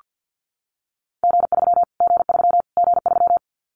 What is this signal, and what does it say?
Morse code, heard as a keyed tone, about 49 wpm, 700 Hz. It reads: G3 G3 G3
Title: Callsign Prefixes for Top 25 DXCC - Repeated 3x - 50wpm